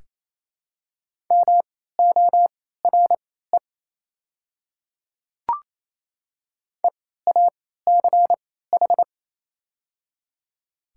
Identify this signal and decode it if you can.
Morse code, heard as a keyed tone, about 27 wpm, spaced as for 16 wpm, 690 Hz. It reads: MORE EACH
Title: 3x Character Spacing - Top 100 Words - 28wpm